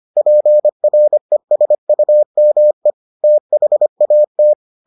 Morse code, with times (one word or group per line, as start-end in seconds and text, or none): 0.2-2.9 PRESUME
3.2-4.5 THAT